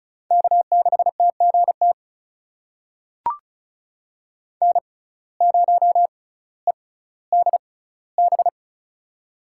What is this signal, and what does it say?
Morse code, heard as a keyed tone, about 36 wpm, spaced as for 18 wpm, 700 Hz. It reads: K6TGT N0EDB
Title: USA Call Signs - 5 Characters - 6x Character Spacing - 35wpm